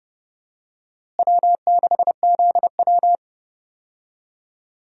Morse code, heard as a keyed tone, about 31 wpm, 700 Hz. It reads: W6ZW
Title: USA Call Signs - 4 Characters - Repeated 2x - 30wpm